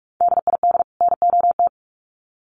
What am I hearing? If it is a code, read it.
Morse code, heard as a keyed tone, about 44 wpm, 700 Hz. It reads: DID NOT
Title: Sets of 2 Words 45wpm